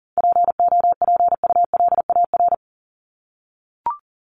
Morse code, heard as a keyed tone, about 40 wpm, 700 Hz. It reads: POPULAR
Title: Top 501-750 Words 40wpm